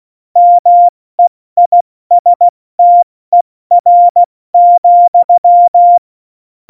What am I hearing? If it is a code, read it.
Morse code, heard as a keyed tone, about 16 wpm, 700 Hz. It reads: MEISTER,
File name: Morsekurs / Witze / Witz 26. Januar 2021, 70 F80